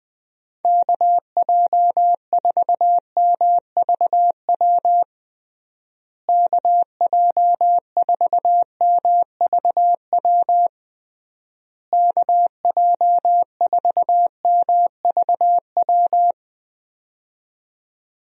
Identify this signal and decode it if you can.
Morse code, heard as a keyed tone, about 21 wpm, 700 Hz. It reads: KJ4MVW KJ4MVW KJ4MVW
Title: USA Call Signs - 6 Characters - Repeated 3x - 20wpm